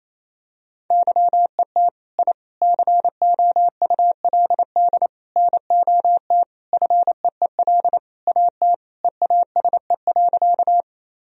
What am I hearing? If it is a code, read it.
Morse code, heard as a keyed tone, about 28 wpm, 700 Hz. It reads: YET I COULD NOT FEEL AT EASE.
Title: Sentences from Top 1000 Words - Review Part 1 - 28wpm